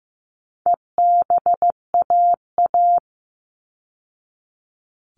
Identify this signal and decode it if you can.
Morse code, heard as a keyed tone, about 15 wpm, 700 Hz. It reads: EBAA
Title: All Cut Numbers - 4-Digit 15wpm